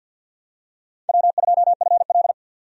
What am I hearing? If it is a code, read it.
Morse code, heard as a keyed tone, about 51 wpm, 700 Hz. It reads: W2FL
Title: USA Call Signs - 4 Characters - 50wpm